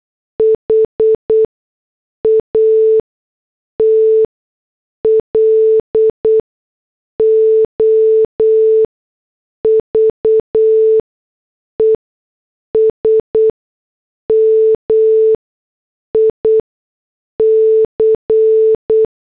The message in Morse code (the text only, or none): HAT LOVES MIC